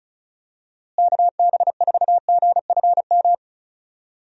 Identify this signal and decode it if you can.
Morse code, heard as a keyed tone, about 34 wpm, 700 Hz. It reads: KB4GFM
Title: USA Call Signs - 6 Characters - Repeated 2x - 35wpm